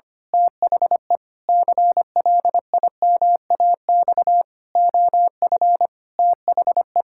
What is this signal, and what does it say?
Morse code, heard as a keyed tone, about 25 wpm, 700 Hz. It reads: THE CLIMAX OF THE
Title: Pangram Challenge v2 - 25wpm